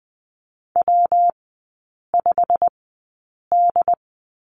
Morse code, heard as a keyed tone, about 20 wpm, spaced as for 8 wpm, 700 Hz. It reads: W5D
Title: 152 Lesson 25e - Letter p - Three Characters 20wpm